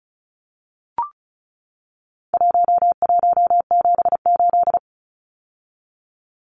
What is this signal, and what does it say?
Morse code, heard as a keyed tone, about 34 wpm, 700 Hz. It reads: 1178
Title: Numbers - 4-Digit 35wpm